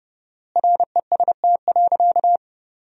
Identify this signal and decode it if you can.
Morse code, heard as a keyed tone, about 29 wpm, 700 Hz. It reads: REST.
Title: 4x Word-Spacing - Sentences from Top 500 Words - 30wpm